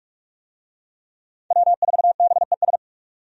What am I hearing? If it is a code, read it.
Morse code, heard as a keyed tone, about 46 wpm, 700 Hz. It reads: W4BES
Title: USA Call Signs - 5 Characters - Repeated 2x - 45wpm